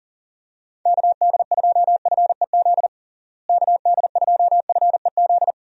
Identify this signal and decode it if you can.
Morse code, heard as a keyed tone, about 40 wpm, 700 Hz. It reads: KD2FEZ KD2FEZ
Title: USA Call Signs - 6 Characters - Repeated 2x - 40wpm